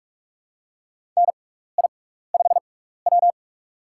Morse code, heard as a keyed tone, about 46 wpm, spaced as for 16 wpm, 700 Hz. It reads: NI5W
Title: USA Call Signs - 4 Characters - 6x Character Spacing - 45wpm